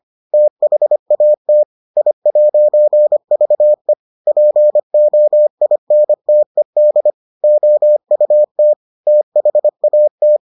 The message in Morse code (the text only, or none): THAT I'VE POINTED OUT THAT